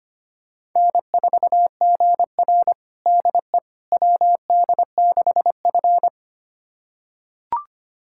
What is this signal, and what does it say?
Morse code, heard as a keyed tone, about 25 wpm, 700 Hz. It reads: N4GR DE WD6F
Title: USA Callsign DE Callsign - 4 Characters - 25wpm